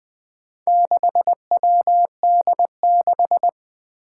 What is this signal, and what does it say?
Morse code, heard as a keyed tone, about 20 wpm, 700 Hz. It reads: 6WD6
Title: Four Challenging Letter-Number Combinations 20wpm